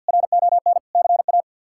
Morse code, heard as a keyed tone, about 46 wpm, 700 Hz. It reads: RON CA
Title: K1USN SST QSO Practice - 50wpm